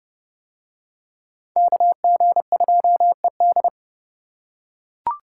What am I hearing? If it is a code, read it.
Morse code, heard as a keyed tone, about 30 wpm, 700 Hz. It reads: KG2ED